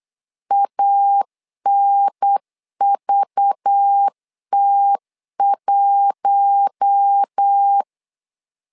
Morse code, heard as a keyed tone, about 8 wpm, 800 Hz. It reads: ANVT1
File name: LEZIONI / 5 LBVC / 5 LBV 8wpm 2837